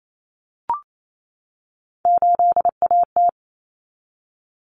Morse code, heard as a keyed tone, about 28 wpm, 700 Hz. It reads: 8AT
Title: Common Cut Numbers - 3-Digit 28wpm